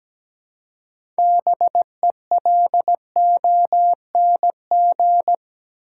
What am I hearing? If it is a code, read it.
Morse code, heard as a keyed tone, about 17 wpm, 700 Hz. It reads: BELONG
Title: Six Letter Words - 17wpm